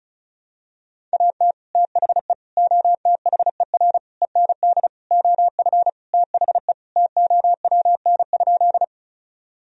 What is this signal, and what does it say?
Morse code, heard as a keyed tone, about 34 wpm, 700 Hz. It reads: AT THE OTHER END OF THE TOWN?
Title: Sentences from Top 1000 Words - Review Part 2 - 35wpm